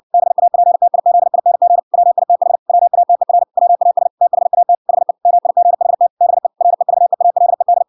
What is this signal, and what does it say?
Morse code, heard as a keyed tone, about 56 wpm, 700 Hz. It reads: BAPTIZING WITH WATER WAS THAT HE MIGHT BE REVEALED